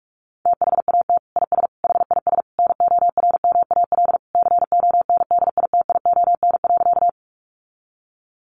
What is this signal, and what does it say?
Morse code, heard as a keyed tone, about 44 wpm, 700 Hz. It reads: THAT IS HIS NORMAL CONDITION.
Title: Sentences from Top 1200 Words 45wpm